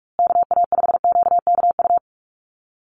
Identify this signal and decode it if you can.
Morse code, heard as a keyed tone, about 44 wpm, 700 Hz. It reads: KA5QKU
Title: USA Call Signs Repeated 2x 45wpm